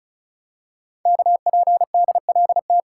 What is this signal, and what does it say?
Morse code, heard as a keyed tone, about 36 wpm, 700 Hz. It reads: KPDLT
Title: Code Groups - 5 Letters - 35wpm